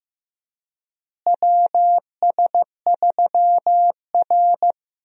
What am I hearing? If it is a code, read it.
Morse code, heard as a keyed tone, about 16 wpm, 700 Hz. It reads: WS3R
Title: USA Call Signs - 4 Characters - Repeated 3x - 15wpm